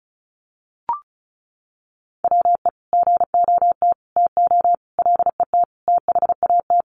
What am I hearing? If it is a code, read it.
Morse code, heard as a keyed tone, about 34 wpm, 700 Hz. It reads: WE GOT TO LET THAT
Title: Sentences from Top 300 Words 35wpm